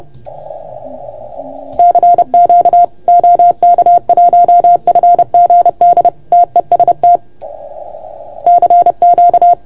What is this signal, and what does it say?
Morse code, heard as a keyed tone, about 30 wpm, 690 Hz. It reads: CQ OK1FGD TEST CQ
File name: Archiv / CUC381 / OK1FGD CUC 381